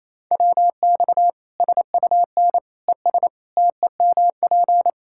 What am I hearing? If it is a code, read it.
Morse code, heard as a keyed tone, about 28 wpm, 700 Hz. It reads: WX SUN ES TEMP
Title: Practice QSO - Exchange 2 - Info - 28wpm